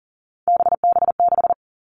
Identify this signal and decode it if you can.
Morse code, heard as a keyed tone, about 37 wpm, 700 Hz. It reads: BB6
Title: Three Challenging Letter-Number Combinations 40wpm